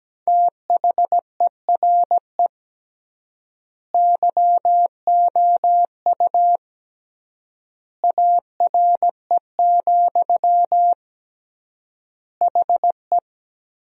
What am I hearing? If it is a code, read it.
Morse code, heard as a keyed tone, about 17 wpm, 700 Hz. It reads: THERE YOU ARE, HE
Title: 3x Word-Spacing - Sentences from Top 100 Words - 17wpm